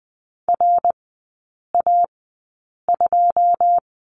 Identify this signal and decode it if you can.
Morse code, heard as a keyed tone, about 20 wpm, spaced as for 8 wpm, 690 Hz. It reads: RA2
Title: 087 Lesson 15e - Number 5 - Three Characters 20wpm